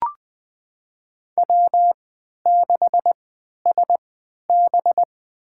Morse code, heard as a keyed tone, about 20 wpm, spaced as for 11 wpm, 700 Hz. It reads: W6SB
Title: USA Call Signs - 4 Characters - 3x Character Spacing - 20wpm